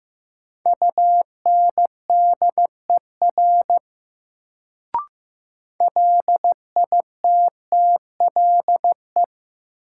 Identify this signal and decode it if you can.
Morse code, heard as a keyed tone, about 15 wpm, 700 Hz. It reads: UNDER LITTLE
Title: Sets of 2 Words - New Word Drill - 15wpm